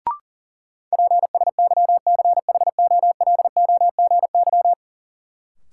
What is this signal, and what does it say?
Morse code, heard as a keyed tone, about 40 wpm, 700 Hz. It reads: PSYCHOLOGY
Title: Sentences from Top 1400 Words - New Word Drill - 40wpm